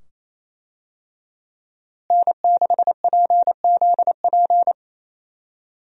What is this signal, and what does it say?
Morse code, heard as a keyed tone, about 28 wpm, 700 Hz. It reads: N6PZP
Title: USA Call Signs - 5 Characters - Repeated 2x - 28wpm